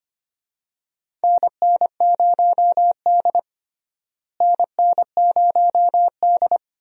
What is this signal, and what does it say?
Morse code, heard as a keyed tone, about 25 wpm, 700 Hz. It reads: NN0D NN0D
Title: USA Call Signs - 4 Characters - Repeated 2x - 25wpm